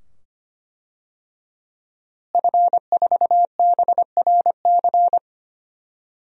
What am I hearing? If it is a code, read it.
Morse code, heard as a keyed tone, about 25 wpm, 700 Hz. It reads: F4BRC
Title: DXCC-100 Call Signs - 5 Characters - Repeated 2x - 25wpm